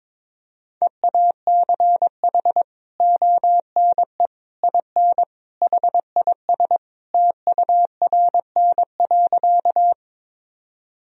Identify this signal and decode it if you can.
Morse code, heard as a keyed tone, about 22 wpm, 700 Hz. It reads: EACH ONE IN HIS TURN.
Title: Sentences from Top 1000 Words - Review Part 1 - 22wpm